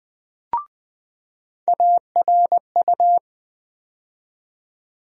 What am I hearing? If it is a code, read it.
Morse code, heard as a keyed tone, about 20 wpm, 700 Hz. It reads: ARU
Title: USA Call Sign Postfixes - 3 Letters 20wpm